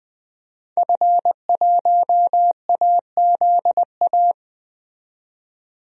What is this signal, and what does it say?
Morse code, heard as a keyed tone, about 20 wpm, 700 Hz. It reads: F1AZA
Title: DXCC-100 Call Signs - 5 Characters - 20wpm